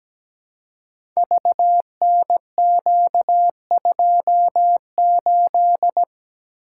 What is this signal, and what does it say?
Morse code, heard as a keyed tone, about 17 wpm, 690 Hz. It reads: VNQ28
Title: Code Groups - 5 Letter:Numbers - 17wpm